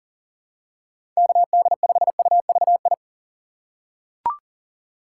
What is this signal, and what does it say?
Morse code, heard as a keyed tone, about 38 wpm, 700 Hz. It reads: KD5UVI
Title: USA Call Signs - 6 Characters - 2x Character Spacing - 40wpm